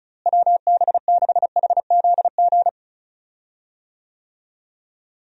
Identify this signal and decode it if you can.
Morse code, heard as a keyed tone, about 36 wpm, 700 Hz. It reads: WB6HZG
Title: USA Call Signs - 6 Characters - Repeated 3x - 35wpm